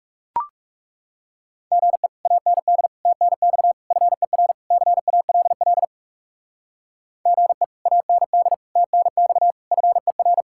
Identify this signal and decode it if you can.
Morse code, heard as a keyed tone, about 38 wpm, 700 Hz. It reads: GE AND TNX FER CALL GE AND TNX FER